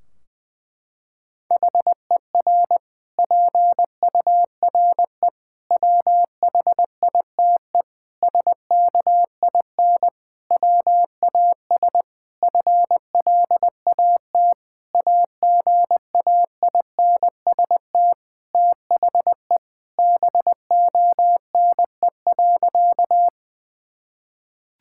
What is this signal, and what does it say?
Morse code, heard as a keyed tone, about 20 wpm, 700 Hz. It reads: HER PURE WHITE SKIN WAS FLAT AGAINST THE BONE.
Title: Sentences from Top 1600 Words 20wpm